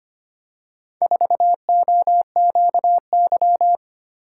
Morse code, heard as a keyed tone, about 25 wpm, 700 Hz. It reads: 4OQY
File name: practice / 25wpm / Four Letter-Number Combinations - 25wpm